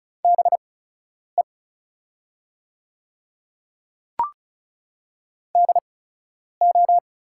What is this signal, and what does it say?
Morse code, heard as a keyed tone, about 34 wpm, spaced as for 10 wpm, 690 Hz. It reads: BE DO
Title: 8x Character Spacing - Top 100 Words - 35wpm